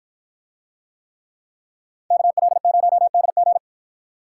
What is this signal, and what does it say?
Morse code, heard as a keyed tone, about 48 wpm, 700 Hz. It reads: KL0DG